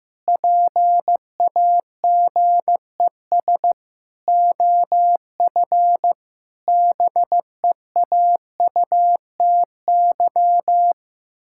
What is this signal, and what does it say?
Morse code, heard as a keyed tone, about 15 wpm, 700 Hz. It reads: PAGES OF BEAUTY
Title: Pangram Challenge v2 - 15wpm